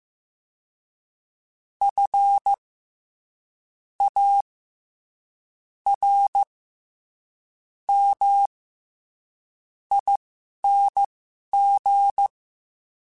Morse code, heard as a keyed tone, about 15 wpm, spaced as for 11 wpm, 800 Hz. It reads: F A R M ING
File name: Special INGs-mp3s / Special INGs-mp3s / SlowToFaster ing @10